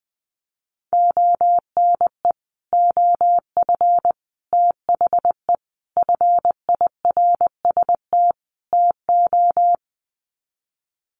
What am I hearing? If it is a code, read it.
Morse code, heard as a keyed tone, about 20 wpm, 700 Hz. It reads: ONE OF THE FIRST TO
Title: Sets of 5 Words Encore 20wpm